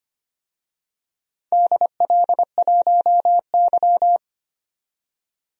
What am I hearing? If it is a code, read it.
Morse code, heard as a keyed tone, about 25 wpm, 700 Hz. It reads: DL1Y
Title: DXCC-100 Call Signs - 4 Characters - 25wpm